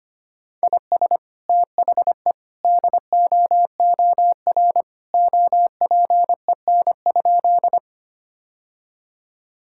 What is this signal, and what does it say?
Morse code, heard as a keyed tone, about 25 wpm, 700 Hz. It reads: IS THE DOOR OPEN?
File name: practice / 25wpm / Sentences with 2-Word Phrases - 25wpm